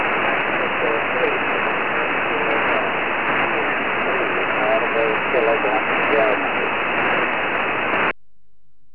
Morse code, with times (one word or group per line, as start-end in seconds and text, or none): none